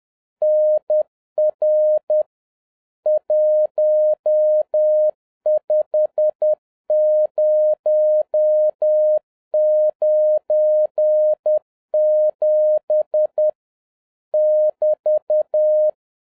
Morse code, heard as a keyed tone, about 10 wpm, 610 Hz. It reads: NR 15097 <BT>